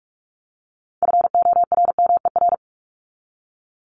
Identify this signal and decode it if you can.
Morse code, heard as a keyed tone, about 44 wpm, 700 Hz. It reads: FORMER
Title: Top 1751-2000 Words 45wpm